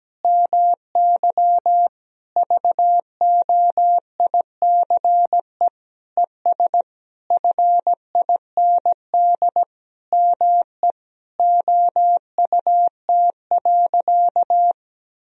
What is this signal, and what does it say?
Morse code, heard as a keyed tone, about 17 wpm, 700 Hz. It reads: MY VOICE ES FIND ME OUT.
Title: Sentences from Top 500 Words and Abbreviations 17wpm